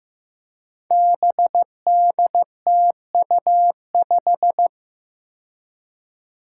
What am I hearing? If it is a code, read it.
Morse code, heard as a keyed tone, about 15 wpm, 700 Hz. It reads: BDTU5
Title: Five Challenging Letter-Number Combinations 15wpm